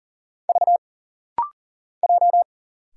Morse code, heard as a keyed tone, about 40 wpm, 700 Hz. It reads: V J
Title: Single Letters - Rapid-Fire 40wpm